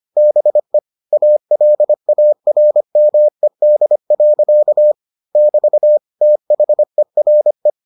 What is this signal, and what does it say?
Morse code, heard as a keyed tone, about 25 wpm, 600 Hz. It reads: BE ALARMED. <BT> THERE